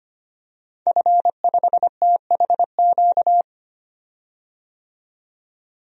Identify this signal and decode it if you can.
Morse code, heard as a keyed tone, about 25 wpm, 700 Hz. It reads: F5THQ
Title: DXCC-100 Call Signs - 5 Characters - 25wpm